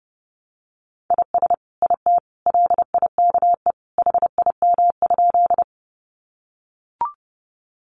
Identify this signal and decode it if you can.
Morse code, heard as a keyed tone, about 30 wpm, 700 Hz. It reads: IS IT LIKE HIM?
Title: Sentences from Top 100 Words 30wpm